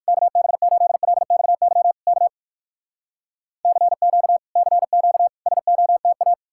Morse code, heard as a keyed tone, about 58 wpm, 700 Hz. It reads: KB8LXY K CQ CQ SOTA